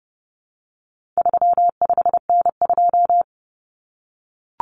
Morse code, heard as a keyed tone, about 30 wpm, 700 Hz. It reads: 35N2
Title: Common Cut Numbers - 4-Digit 30wpm